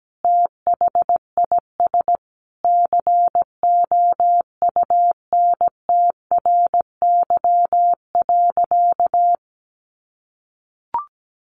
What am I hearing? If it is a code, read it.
Morse code, heard as a keyed tone, about 17 wpm, 700 Hz. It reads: THIS COUNTRY.
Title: Sentences from Top 1200 Words 17wpm